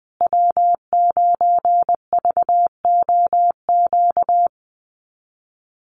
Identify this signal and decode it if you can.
Morse code, heard as a keyed tone, about 20 wpm, 700 Hz. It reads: W9VOQ
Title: USA Call Signs Repeated 3x 20wpm